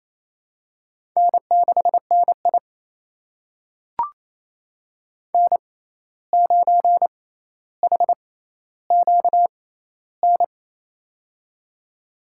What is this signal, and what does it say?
Morse code, heard as a keyed tone, about 28 wpm, spaced as for 13 wpm, 700 Hz. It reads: N6NI N9HQN
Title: USA Call Signs - 6x Character Spacing - 28wpm